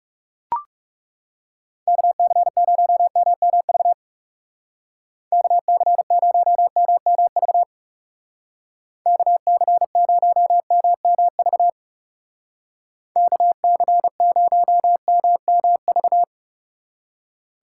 Morse code, heard as a keyed tone, about 36 wpm, 700 Hz. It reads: KC0MMV KC0MMV KC0MMV KC0MMV
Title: Speed Racing - USA 2x3 Call Signs 30wpm